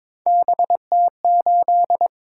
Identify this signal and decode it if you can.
Morse code, heard as a keyed tone, about 22 wpm, 700 Hz. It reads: BT8
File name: practice / 22wpm / Three Letter-Number Combinations - 22wpm